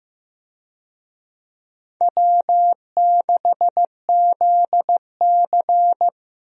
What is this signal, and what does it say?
Morse code, heard as a keyed tone, about 15 wpm, 700 Hz. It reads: W6ZC
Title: USA Call Signs - 4 Characters - Repeated 2x - 15wpm